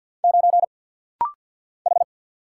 Morse code, heard as a keyed tone, about 50 wpm, 700 Hz. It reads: 9 H
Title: Single Letter-Number - Rapid-Fire 50wpm